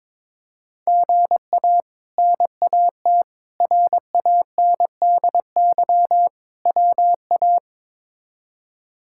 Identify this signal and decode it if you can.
Morse code, heard as a keyed tone, about 22 wpm, 700 Hz. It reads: GA NAT RANDY WA
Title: K1USN SST QSO Practice - 22wpm